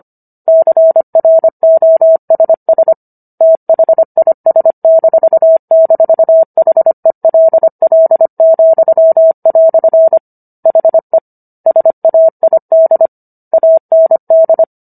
Morse code, heard as a keyed tone, about 25 wpm, 650 Hz. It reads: CROSS THIS--HELL," HE SAID AND
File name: Book One / 16-Chapter 16 - 25 WPM